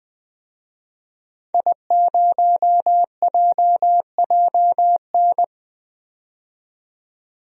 Morse code, heard as a keyed tone, about 20 wpm, 700 Hz. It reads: I0JJN
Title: DXCC-100 Call Signs - 5 Characters - 20wpm